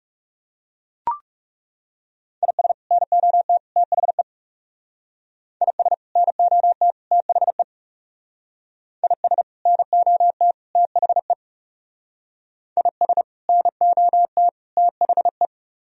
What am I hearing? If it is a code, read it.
Morse code, heard as a keyed tone, about 37 wpm, 700 Hz. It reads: IS NOT THE IS NOT THE IS NOT THE IS NOT THE
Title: Speed-Racing - Sets of 3 Words 30wpm